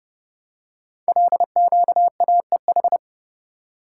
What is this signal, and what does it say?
Morse code, heard as a keyed tone, about 30 wpm, 700 Hz. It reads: LQAEH
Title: Code Groups - 5 Letters - 30wpm